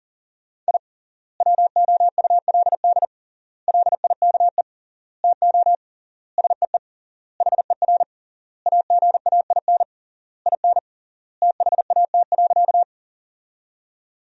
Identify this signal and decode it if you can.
Morse code, heard as a keyed tone, about 40 wpm, 700 Hz. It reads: I WOULD LIKE TO SEE HER AGAIN IN THAT.
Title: 3x Word-Spacing - Sentences from Top 200 Words - 40wpm